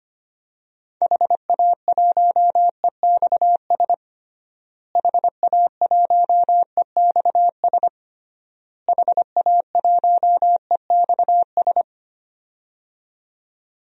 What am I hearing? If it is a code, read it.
Morse code, heard as a keyed tone, about 25 wpm, 700 Hz. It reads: HA1EXS HA1EXS HA1EXS
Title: DXCC-100 Call Signs - 6 Characters - Repeated 3x - 25wpm